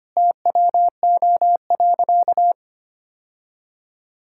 Morse code, heard as a keyed tone, about 25 wpm, 700 Hz. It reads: TWO.
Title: Sentences from Top 1000 Words - Review Part 5 - 25wpm